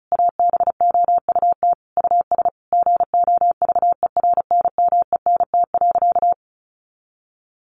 Morse code, heard as a keyed tone, about 34 wpm, 700 Hz. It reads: ABOUT US GOVERNMENT.
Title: Easy Phrases 35wpm